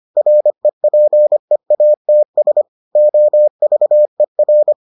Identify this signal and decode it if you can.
Morse code, heard as a keyed tone, about 25 wpm, 600 Hz. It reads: REPEATS OVER